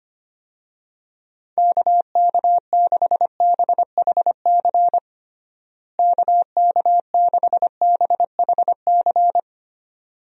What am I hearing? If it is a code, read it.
Morse code, heard as a keyed tone, about 25 wpm, 700 Hz. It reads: KK6BHC KK6BHC
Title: USA Call Signs - 6 Characters - Repeated 2x - 25wpm